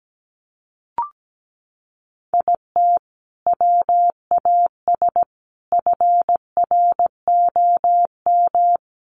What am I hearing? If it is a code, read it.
Morse code, heard as a keyed tone, about 17 wpm, 700 Hz. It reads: IT WAS FROM
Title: Sentences from Top 100 Words 17wpm